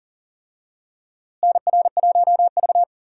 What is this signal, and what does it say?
Morse code, heard as a keyed tone, about 40 wpm, 700 Hz. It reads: NR1V
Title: USA Call Signs - 4 Characters - Repeated 2x - 40wpm